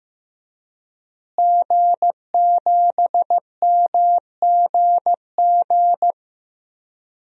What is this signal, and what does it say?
Morse code, heard as a keyed tone, about 15 wpm, 700 Hz. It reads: G7MGG
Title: DXCC-100 Call Signs - 5 Characters - Repeated 2x - 15wpm